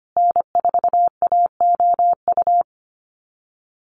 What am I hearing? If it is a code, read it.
Morse code, heard as a keyed tone, about 25 wpm, 700 Hz. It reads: N4AOU
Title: USA Call Signs Repeated 2x 25wpm